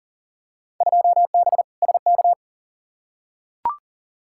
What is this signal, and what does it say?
Morse code, heard as a keyed tone, about 40 wpm, 700 Hz. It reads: 2B SK
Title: Field Day Exchanges Encore - 40wpm